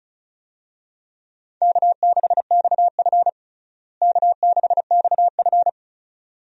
Morse code, heard as a keyed tone, about 36 wpm, 700 Hz. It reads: K6XF K6XF
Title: USA Call Signs - 4 Characters - Repeated 2x - 35wpm